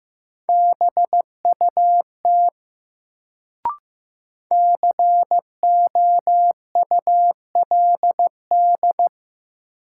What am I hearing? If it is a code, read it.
Morse code, heard as a keyed tone, about 15 wpm, 700 Hz. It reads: BUT COULD
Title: Sentences from Top 100 Words - New Word Drill - 15wpm